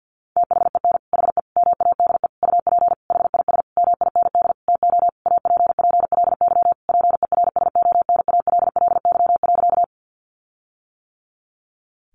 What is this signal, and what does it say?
Morse code, heard as a keyed tone, about 50 wpm, 700 Hz. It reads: THEN HE MADE UP HIS MIND TO APPLY PERSONALLY.
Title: Sentences from Top 1400 Words 50wpm